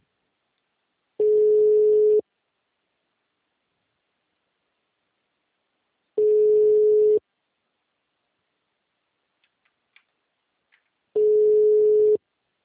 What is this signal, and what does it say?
Ringback tone (Europe, 425 Hz).
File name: out- -2011- - - .55778